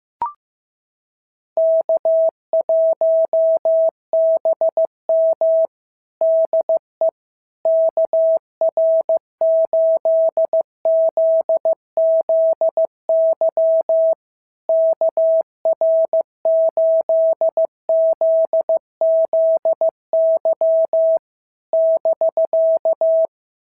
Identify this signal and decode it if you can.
Morse code, heard as a keyed tone, about 15 wpm, 650 Hz. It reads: K1BM DE KR8ZZY KR8ZZY <BK>
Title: SOTA QSO Practice 15wpm